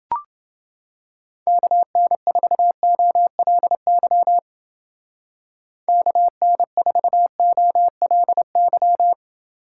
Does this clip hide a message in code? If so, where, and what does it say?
Yes: Morse code from 1.5 to 9.1 s, heard as a keyed tone, about 28 wpm, 700 Hz. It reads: KN4OLY KN4OLY